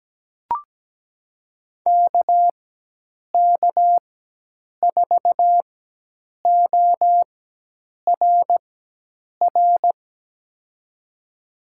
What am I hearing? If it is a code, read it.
Morse code, heard as a keyed tone, about 17 wpm, spaced as for 8 wpm, 700 Hz. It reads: KK4ORR
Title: USA Call Signs - 6 Characters - 4x Character Spacing - 17wpm